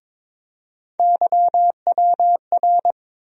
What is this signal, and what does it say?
Morse code, heard as a keyed tone, about 22 wpm, 700 Hz. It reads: YWR
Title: USA Call Sign Postfixes - 3 Letters 22wpm